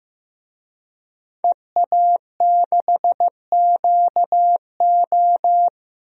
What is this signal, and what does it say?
Morse code, heard as a keyed tone, about 15 wpm, 700 Hz. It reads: EA6QO
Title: DXCC-100 Call Signs - 5 Characters - 15wpm